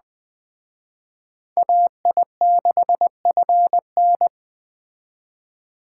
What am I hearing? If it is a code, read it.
Morse code, heard as a keyed tone, about 20 wpm, 700 Hz. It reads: AI6FN
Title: Speed Racing - USA 2x2 Call Signs 17wpm